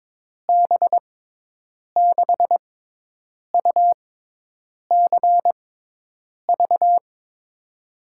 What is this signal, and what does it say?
Morse code, heard as a keyed tone, about 22 wpm, 700 Hz. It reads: B 6 U C V
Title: USA Call Signs - 6x Character Spacing - 22wpm